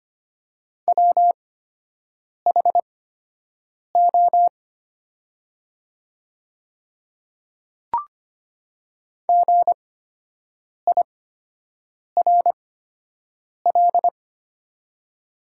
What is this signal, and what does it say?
Morse code, heard as a keyed tone, about 25 wpm, spaced as for 7 wpm, 700 Hz. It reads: WHO GIRL
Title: 8x Character Spacing - Top 500 Words - 25wpm